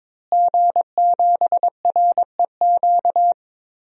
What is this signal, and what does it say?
Morse code, heard as a keyed tone, about 22 wpm, 700 Hz. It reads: G7REQ
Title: DXCC-100 Call Signs - 5 Characters - Repeated 3x - 22wpm